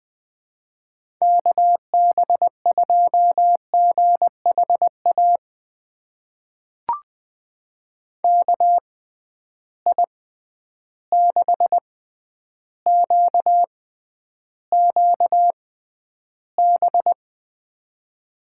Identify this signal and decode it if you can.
Morse code, heard as a keyed tone, about 20 wpm, spaced as for 10 wpm, 700 Hz. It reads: KB2GHA KI6QQB
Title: USA Call Signs - 6 Characters - 6x Character Spacing - 20wpm